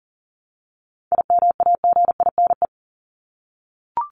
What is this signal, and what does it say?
Morse code, heard as a keyed tone, about 40 wpm, 700 Hz. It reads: IMAGINE
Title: Top 1251-1500 Words 40wpm